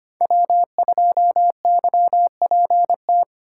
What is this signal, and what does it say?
Morse code, heard as a keyed tone, about 25 wpm, 700 Hz. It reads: W2YPT
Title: USA Call Signs - 5 Characters - 3x Character Spacing - 25wpm